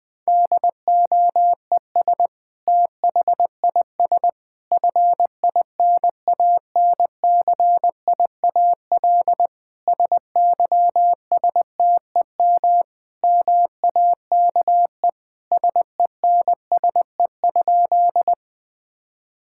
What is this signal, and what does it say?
Morse code, heard as a keyed tone, about 21 wpm, 700 Hz. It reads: DOES THIS FINANCIAL SYSTEM MAKE SENSE?
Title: Sentences from Top 1000 Words - Review Part 5 - 20wpm